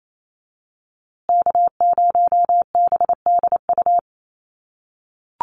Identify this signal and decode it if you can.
Morse code, heard as a keyed tone, about 28 wpm, 700 Hz. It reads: K0BDU
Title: USA Call Signs Repeated 3x 28wpm